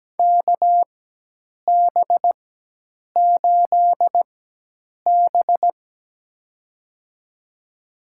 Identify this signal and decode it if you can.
Morse code, heard as a keyed tone, about 17 wpm, spaced as for 8 wpm, 700 Hz. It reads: KB8B
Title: USA Call Signs - 4 Characters - 4x Character Spacing - 17wpm